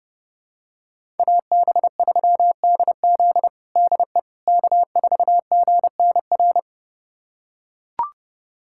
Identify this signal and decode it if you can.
Morse code, heard as a keyed tone, about 30 wpm, 700 Hz. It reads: AB3DZ DE K4GNR
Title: USA Callsign DE Callsign - 5 Characters - 30wpm